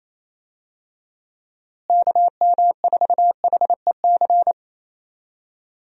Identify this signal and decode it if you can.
Morse code, heard as a keyed tone, about 28 wpm, 700 Hz. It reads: KM4HEC
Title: USA Call Signs - 6 Characters - Repeated 2x - 28wpm